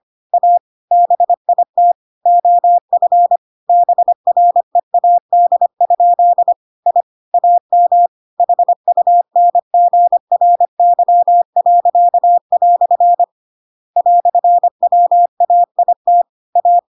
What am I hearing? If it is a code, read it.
Morse code, heard as a keyed tone, about 25 wpm, 700 Hz. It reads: A BIT OF BREAD? I AM HUNGRY." "WAIT A